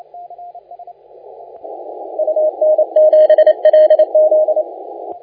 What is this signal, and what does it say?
Morse code, heard as a keyed tone, about 28 wpm, 620 Hz. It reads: UN7LZ
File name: la4c / sac-cw-2018 / CONTEST AUDIO 1600Z